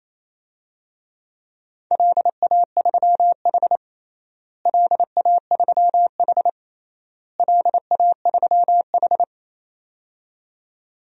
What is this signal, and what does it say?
Morse code, heard as a keyed tone, about 28 wpm, 700 Hz. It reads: LA3H LA3H LA3H
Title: DXCC-100 Call Signs - 4 Characters - Repeated 3x - 28wpm